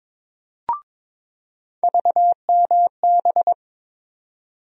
Morse code, heard as a keyed tone, about 23 wpm, 700 Hz. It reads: VMB
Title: USA Call Sign Postfixes - 3 Letters 22wpm